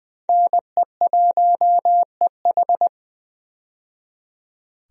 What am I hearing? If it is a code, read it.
Morse code, heard as a keyed tone, about 20 wpm, 700 Hz. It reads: NE1EH
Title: USA Call Signs - 5 Characters - 20wpm